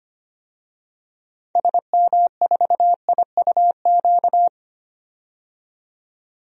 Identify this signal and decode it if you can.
Morse code, heard as a keyed tone, about 25 wpm, 700 Hz. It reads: SM4IUQ
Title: DXCC-100 Call Signs - 6 Characters - 25wpm